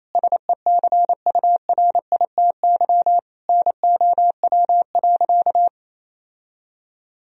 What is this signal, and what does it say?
Morse code, heard as a keyed tone, about 28 wpm, 700 Hz. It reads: SECURITY NOW.
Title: Sentences from Top 1000 Words - Review Part 4 - 28wpm